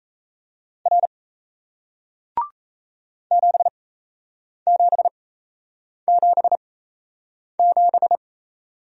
Morse code, heard as a keyed tone, about 37 wpm, spaced as for 8 wpm, 700 Hz. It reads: R 7777